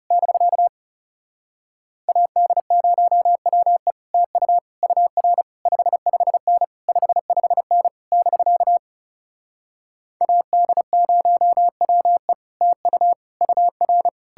Morse code, heard as a keyed tone, about 37 wpm, 700 Hz. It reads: <BK> AD0WE TU UR 55N 55N <BK> AD0WE TU UR
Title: Speed-Racing Common QSO Phrases Encore - 30wpm